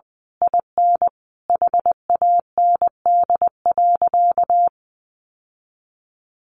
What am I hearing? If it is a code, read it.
Morse code, heard as a keyed tone, about 19 wpm, 700 Hz. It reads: IN HAND.
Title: Sentences from Top 1200 Words 20wpm